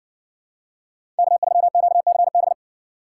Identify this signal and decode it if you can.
Morse code, heard as a keyed tone, about 60 wpm, 700 Hz. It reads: K3QCB